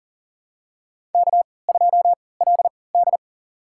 Morse code, heard as a keyed tone, about 40 wpm, spaced as for 23 wpm, 700 Hz. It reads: K2LD
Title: USA Call Signs - 4 Characters - 3x Character Spacing - 40wpm